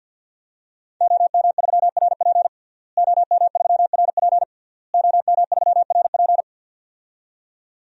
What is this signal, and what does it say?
Morse code, heard as a keyed tone, about 51 wpm, 700 Hz. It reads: OM3RP OM3RP OM3RP
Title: DXCC-100 Call Signs - 5 Characters - Repeated 3x - 50wpm